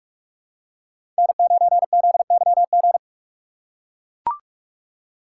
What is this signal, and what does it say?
Morse code, heard as a keyed tone, about 41 wpm, 700 Hz. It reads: N9ZYG
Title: USA Call Signs - 2x Character Spacing - 45wpm